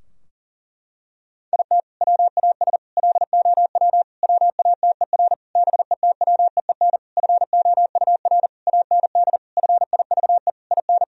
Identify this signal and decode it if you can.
Morse code, heard as a keyed tone, about 40 wpm, 700 Hz. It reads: IT WAS LOW WATER BETWEEN FOUR AND FIVE IN
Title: Sentences from Top 1000 Words - Review Part 2 - 40wpm